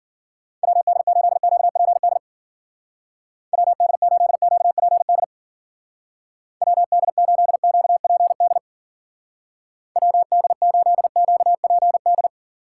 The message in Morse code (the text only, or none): WD8QPD WD8QPD WD8QPD WD8QPD